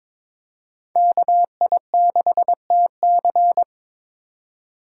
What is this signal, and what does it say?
Morse code, heard as a keyed tone, about 22 wpm, 700 Hz. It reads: KI6TC
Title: USA Call Signs - 5 Characters - 3x Character Spacing - 22wpm